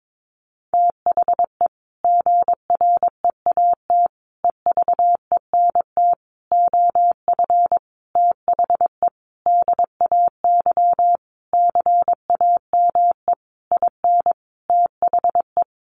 Morse code, heard as a keyed tone, about 22 wpm, 700 Hz. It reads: THE GREAT EVENT OF THE DAY CAME IN THE
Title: Sentences from Top 1500 Words 22wpm